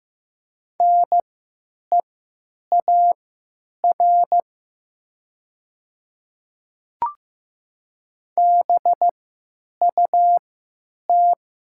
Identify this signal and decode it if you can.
Morse code, heard as a keyed tone, about 15 wpm, spaced as for 9 wpm, 700 Hz. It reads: NEAR BUT
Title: 3x Character Spacing - Top 200 Words - 15wpm